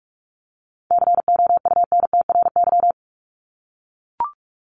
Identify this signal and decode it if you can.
Morse code, heard as a keyed tone, about 44 wpm, 700 Hz. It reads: COUNTRY
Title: Top 500 Words Repeated 45wpm